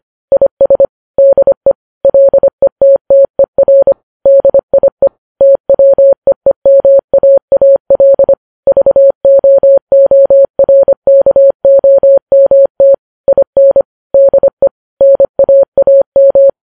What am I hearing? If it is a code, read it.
Morse code, heard as a keyed tone, about 25 wpm, 550 Hz. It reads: IS DE LETTER DIE TWEEMAAL VOORKOMT IN DE NAAM
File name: Files Web tls / Morse on9fv / morse25kps